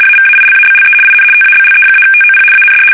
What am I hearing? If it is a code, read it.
RTTY, about 75 baud. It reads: AVAILABLE IN TABLE-TOP AND
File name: fsk850-75B / teletype-corp